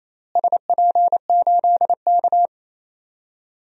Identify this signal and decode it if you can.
Morse code, heard as a keyed tone, about 27 wpm, 700 Hz. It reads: SP8K
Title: DXCC-100 Call Signs - 4 Characters - Repeated 3x - 28wpm